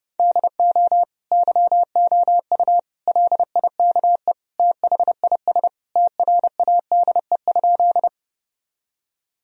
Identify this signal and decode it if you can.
Morse code, heard as a keyed tone, about 30 wpm, 700 Hz. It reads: DO YOU LIKE THIS TRADE?
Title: Sentences from Top 1000 Words - Review Part 4 - 30wpm